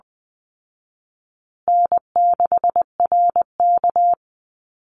Morse code, heard as a keyed tone, about 20 wpm, 700 Hz. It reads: N6RK
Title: Easy Call Signs Repeated 3x - 20wpm